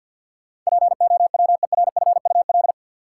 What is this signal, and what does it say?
Morse code, heard as a keyed tone, about 53 wpm, 700 Hz. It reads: POWERFUL
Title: Sentences from Top 1000 Words - Review Part 6 - New Word Drill - 50wpm